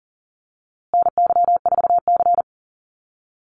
Morse code, heard as a keyed tone, about 40 wpm, 700 Hz. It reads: NY4C
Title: USA Call Signs Repeated 3x 40wpm